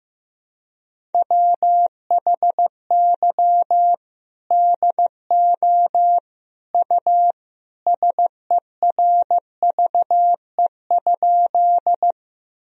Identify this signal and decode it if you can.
Morse code, heard as a keyed tone, about 15 wpm, 700 Hz. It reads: WHY DO U SERVE?
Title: Sentences from Top 400 Words and Abbreviations 15wpm